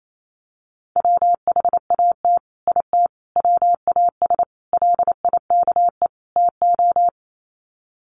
Morse code, heard as a keyed tone, about 28 wpm, 700 Hz. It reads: WHAT IT WAS LIKE TO
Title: Sets of 5 Words Encore 28wpm